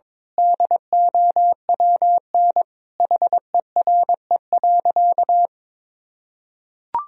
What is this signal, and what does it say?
Morse code, heard as a keyed tone, about 22 wpm, 700 Hz. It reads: DOWN HERE.
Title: Sentences from Top 1000 Words - Review Part 3 - 22wpm